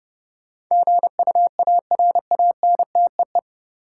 Morse code, heard as a keyed tone, about 32 wpm, 700 Hz. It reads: GUARANTEE
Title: Sentences from Top 1600 Words - New Word Drill - 30wpm